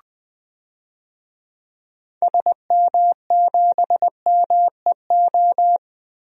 Morse code, heard as a keyed tone, about 20 wpm, 700 Hz. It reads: SM7MEO
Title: DXCC-100 Call Signs - 6 Characters - Repeated 3x - 20wpm